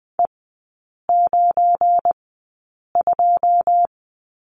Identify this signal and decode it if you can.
Morse code, heard as a keyed tone, about 20 wpm, spaced as for 8 wpm, 700 Hz. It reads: E92
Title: 191 Lesson 31e - Letter v - Three Characters 20wpm